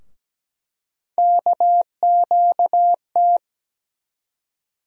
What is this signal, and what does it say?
Morse code, heard as a keyed tone, about 17 wpm, 700 Hz. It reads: KQT
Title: USA Call Sign Postfixes - 3 Letters 17wpm